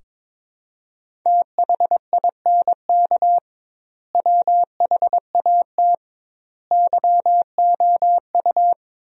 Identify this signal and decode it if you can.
Morse code, heard as a keyed tone, about 22 wpm, 700 Hz. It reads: THINK WHAT YOU
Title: 2x Word-Spacing - Sentences from Top 200 Words - 22wpm